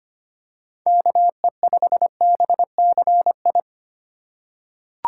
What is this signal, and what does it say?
Morse code, heard as a keyed tone, about 25 wpm, 700 Hz. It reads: KE5BCI
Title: USA Call Signs - 6 Characters - 4x Character Spacing - 25wpm